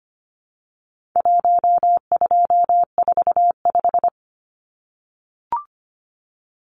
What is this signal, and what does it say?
Morse code, heard as a keyed tone, about 25 wpm, 700 Hz. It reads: 1245
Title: Numbers - Contest Style - 4-Digit 25wpm